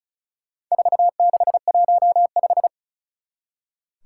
Morse code, heard as a keyed tone, about 36 wpm, 700 Hz. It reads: 4615
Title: Four Challenging Letter-Number Combinations 35wpm